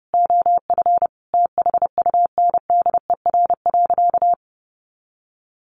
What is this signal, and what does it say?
Morse code, heard as a keyed tone, about 30 wpm, 700 Hz. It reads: OF THUNDER.
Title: A Sound Of Thunder 30wpm